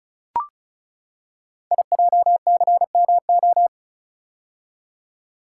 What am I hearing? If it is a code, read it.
Morse code, heard as a keyed tone, about 36 wpm, 690 Hz. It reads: IJCMO
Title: Code Groups - 5 Letters - 35wpm